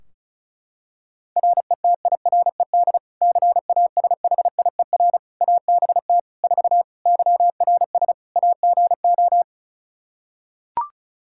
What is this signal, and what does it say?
Morse code, heard as a keyed tone, about 34 wpm, 700 Hz. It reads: RETIRED CASHIER ABT 4 YRS AGO
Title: Practice QSO - Exchange 3 - More Info - 35wpm